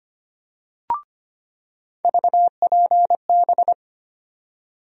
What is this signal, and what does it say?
Morse code, heard as a keyed tone, about 25 wpm, 700 Hz. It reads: VPB
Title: USA Call Sign Postfixes - 3 Letters 25wpm